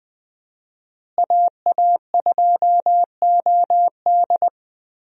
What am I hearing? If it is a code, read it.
Morse code, heard as a keyed tone, about 20 wpm, 700 Hz. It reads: AA2OD
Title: USA Call Signs - 5 Characters - Repeated 3x - 20wpm